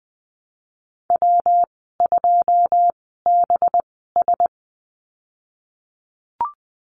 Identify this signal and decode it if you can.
Morse code, heard as a keyed tone, about 20 wpm, spaced as for 15 wpm, 700 Hz. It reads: W2BS
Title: 219 Lesson 35g - Number 0 - Call Signs 20wpm